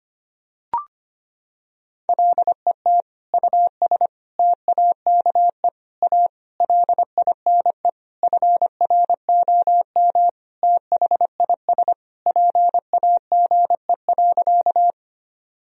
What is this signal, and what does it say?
Morse code, heard as a keyed tone, about 25 wpm, 700 Hz. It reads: LET US TAKE A LINE FROM THIS PAGE.
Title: Sentences from Top 1000 Words - Review Part 1 - 25wpm